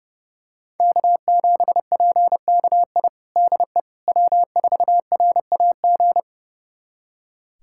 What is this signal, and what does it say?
Morse code, heard as a keyed tone, about 32 wpm, 700 Hz. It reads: K7PKI DE W4RAG
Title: USA Callsign DE Callsign - 5 Characters - 30wpm